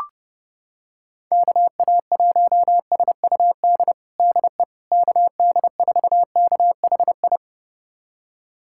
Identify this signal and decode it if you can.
Morse code, heard as a keyed tone, about 29 wpm, 700 Hz. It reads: KA1SUD DE KD4KHI
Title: USA Callsign DE Callsign - 6 Characters - 30wpm